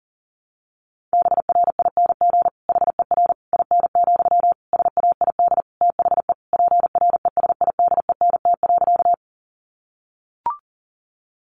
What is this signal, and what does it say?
Morse code, heard as a keyed tone, about 40 wpm, 700 Hz. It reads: BRING HER IN, SAID THE PRESIDENT.
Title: Sentences from Top 1000 Words 40wpm